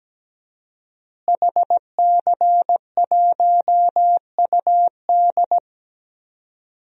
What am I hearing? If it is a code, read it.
Morse code, heard as a keyed tone, about 17 wpm, 700 Hz. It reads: HC1UD
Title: DXCC-100 Call Signs - 5 Characters - Repeated 3x - 17wpm